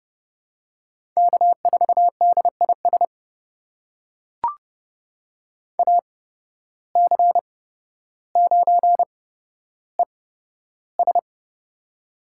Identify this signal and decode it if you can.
Morse code, heard as a keyed tone, about 30 wpm, spaced as for 13 wpm, 700 Hz. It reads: K4DIS AC9ES
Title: USA Call Signs - 8x Character Spacing - 30wpm